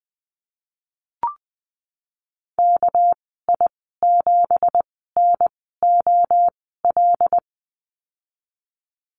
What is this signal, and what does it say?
Morse code, heard as a keyed tone, about 20 wpm, spaced as for 15 wpm, 700 Hz. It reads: KI7NOL
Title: 231 Lesson 37g - Letter x - Call Signs 20wpm